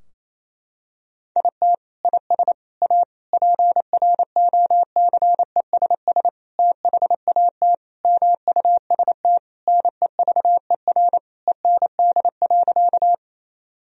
Morse code, heard as a keyed tone, about 28 wpm, 700 Hz. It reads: IT IS A PROCESS THAT MUST NEVER END.
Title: Sentences from Top 1000 Words - Review Part 3 - 28wpm